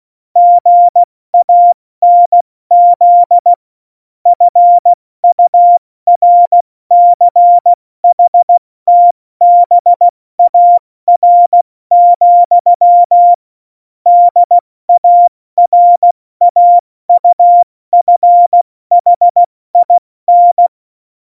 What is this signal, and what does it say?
Morse code, heard as a keyed tone, about 16 wpm, 700 Hz. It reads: GANZ FURCHTBAR, DARAUFHIN
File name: Morsekurs / Witze / Witz 27 Juni 17, Tempo 70 F80